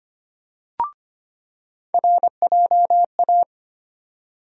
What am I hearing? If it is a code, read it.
Morse code, heard as a keyed tone, about 25 wpm, 700 Hz. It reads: RJA
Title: USA Call Sign Postfixes - 3 Letters 25wpm